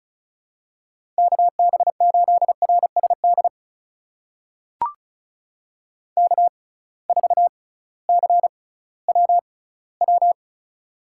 Morse code, heard as a keyed tone, about 34 wpm, spaced as for 19 wpm, 700 Hz. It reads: KB8RSD K4CWW
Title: USA Call Signs - 6x Character Spacing - 35wpm